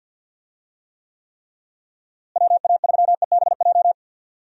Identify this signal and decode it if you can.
Morse code, heard as a keyed tone, about 52 wpm, 700 Hz. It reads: WA3EBJ
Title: USA Call Signs - 6 Characters - Repeated 3x - 50wpm